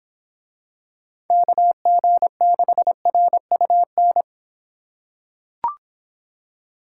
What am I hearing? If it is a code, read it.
Morse code, heard as a keyed tone, about 24 wpm, 700 Hz. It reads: KG6RUN